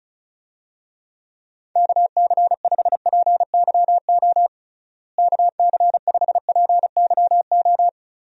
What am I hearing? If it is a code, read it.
Morse code, heard as a keyed tone, about 34 wpm, 700 Hz. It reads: KC5PYO KC5PYO
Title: USA Call Signs - 6 Characters - Repeated 3x - 35wpm